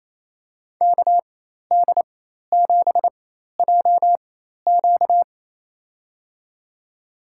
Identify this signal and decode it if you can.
Morse code, heard as a keyed tone, about 28 wpm, spaced as for 13 wpm, 700 Hz. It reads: KD7JQ
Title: USA Call Signs - 5 Characters - 4x Character Spacing - 28wpm